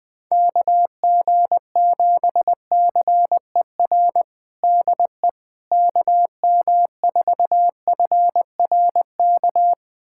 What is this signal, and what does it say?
Morse code, heard as a keyed tone, about 20 wpm, 700 Hz. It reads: KG7CER DE KM4FRK
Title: USA Callsign DE Callsign 20wpm